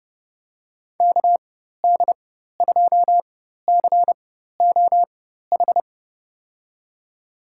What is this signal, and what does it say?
Morse code, heard as a keyed tone, about 31 wpm, spaced as for 14 wpm, 700 Hz. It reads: KD2COH
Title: USA Call Signs - 4x Character Spacing - 30wpm